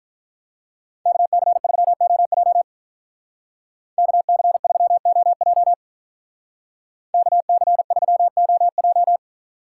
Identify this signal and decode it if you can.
Morse code, heard as a keyed tone, about 45 wpm, 700 Hz. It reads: KC3OJ KC3OJ KC3OJ